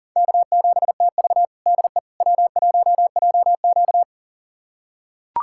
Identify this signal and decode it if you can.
Morse code, heard as a keyed tone, about 37 wpm, 700 Hz. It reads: K7TV DE W1JQ
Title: USA Callsign DE Callsign - 4 Characters - 40wpm